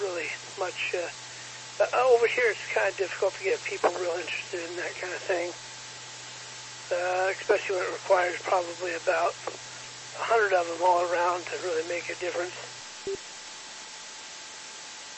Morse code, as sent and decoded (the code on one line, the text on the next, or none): none